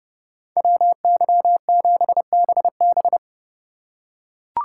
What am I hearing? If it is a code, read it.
Morse code, heard as a keyed tone, about 29 wpm, 700 Hz. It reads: WY7BB